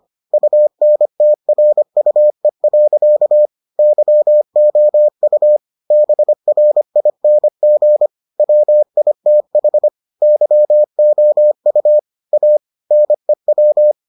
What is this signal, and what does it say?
Morse code, heard as a keyed tone, about 25 wpm, 600 Hz. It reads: UNTRUE. YOU BRING WITH YOU A NEW